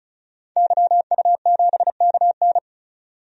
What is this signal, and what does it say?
Morse code, heard as a keyed tone, about 36 wpm, 700 Hz. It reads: YU7KN
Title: DXCC-100 Call Signs - 5 Characters - 35wpm